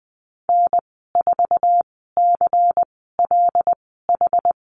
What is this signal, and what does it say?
Morse code, heard as a keyed tone, about 20 wpm, spaced as for 15 wpm, 700 Hz. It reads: N4CLH
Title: 147 Lesson 24g - Letter y - Call Signs 20wpm